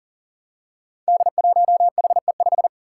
Morse code, heard as a keyed tone, about 40 wpm, 700 Hz. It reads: D1HE5
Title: Five Challenging Letter-Number Combinations 40wpm